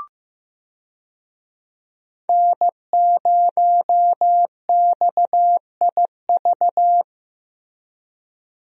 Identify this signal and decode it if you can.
Morse code, heard as a keyed tone, about 15 wpm, 700 Hz. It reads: N0XIV
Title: USA Call Signs - 5 Characters - Repeated 3x - 15wpm